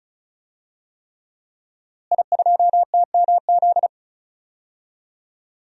Morse code, heard as a keyed tone, about 34 wpm, 700 Hz. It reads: I2TMZ
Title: DXCC-100 Call Signs - 5 Characters - Repeated 3x - 35wpm